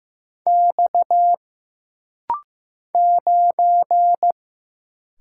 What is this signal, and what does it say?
Morse code, heard as a keyed tone, about 15 wpm, 700 Hz. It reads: X 9
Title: Single Letter-Number Alt - Rapid-Fire 15wpm